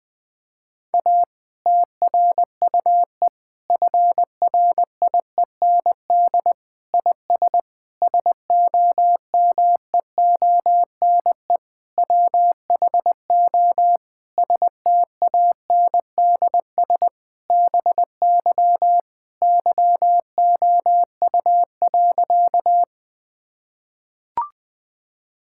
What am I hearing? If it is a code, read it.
Morse code, heard as a keyed tone, about 20 wpm, 700 Hz. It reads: A TRUE FRIEND IS SOMEONE WHO STANDS BY YOU.
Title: Difficult Sentences with Hints - 20wpm